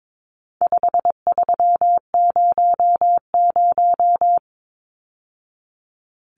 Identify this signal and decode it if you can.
Morse code, heard as a keyed tone, about 22 wpm, 700 Hz. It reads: 5300
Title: Numbers - 4-Digit 22wpm